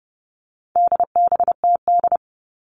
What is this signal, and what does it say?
Morse code, heard as a keyed tone, about 30 wpm, 700 Hz. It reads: DBTD
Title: All Cut Numbers - 4-Digit 30wpm